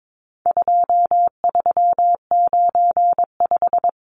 Morse code, heard as a keyed tone, about 22 wpm, 700 Hz. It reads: 2395
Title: Numbers - 4-Digit 22wpm